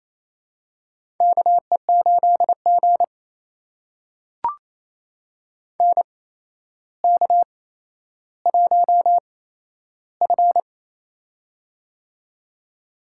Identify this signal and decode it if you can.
Morse code, heard as a keyed tone, about 28 wpm, spaced as for 12 wpm, 700 Hz. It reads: KE8G NK1F
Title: USA Call Signs - 4 Characters - 8x Character Spacing - 28wpm